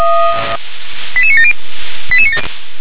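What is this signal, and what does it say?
Five-tone selective call (ZVEI), heard as quick runs of five tones, tones of about 70 ms. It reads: 88970
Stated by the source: ZVEI five-tone selective call received on SDR.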